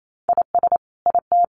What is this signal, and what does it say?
Morse code, heard as a keyed tone, about 28 wpm, 700 Hz. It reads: IS IT
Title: Sentences from Top 500 Words 28wpm